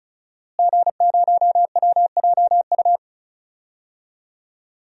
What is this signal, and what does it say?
Morse code, heard as a keyed tone, about 34 wpm, 700 Hz. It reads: G0WJU
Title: DXCC-100 Call Signs - 5 Characters - Repeated 3x - 35wpm